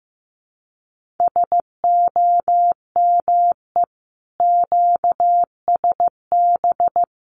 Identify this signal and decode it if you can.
Morse code, heard as a keyed tone, about 14 wpm, 700 Hz. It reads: SOME QSB
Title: Common QSO Phrases 15wpm